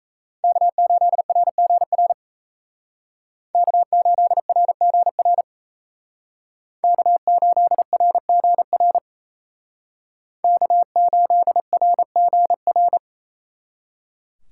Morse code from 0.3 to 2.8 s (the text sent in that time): K8RGR